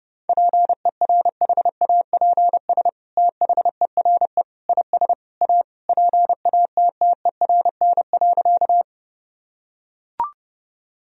Morse code, heard as a keyed tone, about 30 wpm, 700 Hz. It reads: PERHAPS THERE IS A PATTERN.
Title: Sentences from Top 1000 Words - Review Part 2 - 30wpm